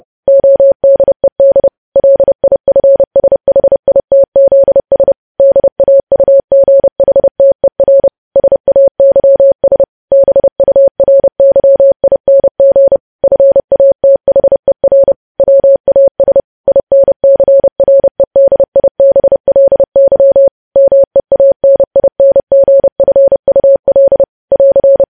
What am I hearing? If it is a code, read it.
Morse code, heard as a keyed tone, about 30 wpm, 550 Hz. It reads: ODED LIFSHITZS DAUGHTER SAYS BURYING FATHER WAS INCREDIBLY MEANINGFUL <AR>
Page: News Headlines in Morse Code at 30 WPM